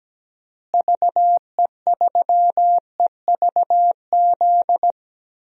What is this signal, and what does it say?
Morse code, heard as a keyed tone, about 17 wpm, 700 Hz. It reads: VE3EVZ
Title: DXCC-100 Call Signs - 6 Characters - Repeated 3x - 17wpm